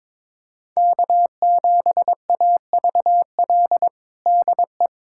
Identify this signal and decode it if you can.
Morse code, heard as a keyed tone, about 22 wpm, 700 Hz. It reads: K7AVL DE
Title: USA Callsign DE Callsign - 5 Characters - 22wpm